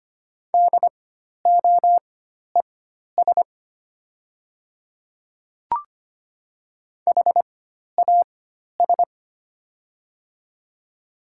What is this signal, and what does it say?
Morse code, heard as a keyed tone, about 25 wpm, spaced as for 12 wpm, 700 Hz. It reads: DOES HAS
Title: 4x Character Spacing - Top 200 Words - 25wpm